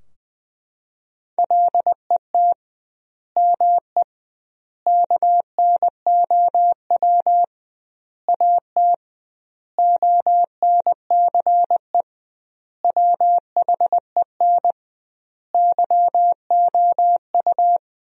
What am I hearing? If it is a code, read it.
Morse code, heard as a keyed tone, about 21 wpm, 700 Hz. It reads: LET ME KNOW AT ONCE WHEN YOU
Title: 2x Word-Spacing - Sentences from Top 300 Words - 20wpm